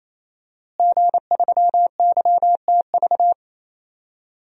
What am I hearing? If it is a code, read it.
Morse code, heard as a keyed tone, about 28 wpm, 700 Hz. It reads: G3YTV
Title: DXCC-100 Call Signs - 5 Characters - Repeated 3x - 28wpm